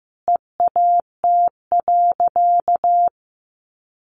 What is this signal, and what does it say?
Morse code, heard as a keyed tone, about 15 wpm, 700 Hz. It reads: EAT.
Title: Sentences from Top 800 Words 15wpm